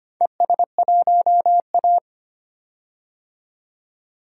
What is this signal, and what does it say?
Morse code, heard as a keyed tone, about 25 wpm, 700 Hz. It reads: ES1A
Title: DXCC-100 Call Signs - 4 Characters - 25wpm